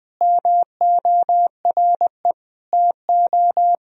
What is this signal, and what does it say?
Morse code, heard as a keyed tone, about 20 wpm, 700 Hz. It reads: MORE TO
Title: Sentences from Top 1000 Words - Review Part 1 - 20wpm